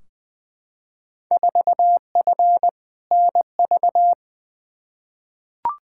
Morse code, heard as a keyed tone, about 21 wpm, 700 Hz. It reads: 4F NV
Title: Field Day Exchanges Encore - 20wpm